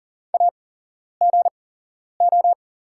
Morse code, heard as a keyed tone, about 38 wpm, spaced as for 11 wpm, 690 Hz. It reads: AGO
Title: 8x Character Spacing - Top 500 Words - 40wpm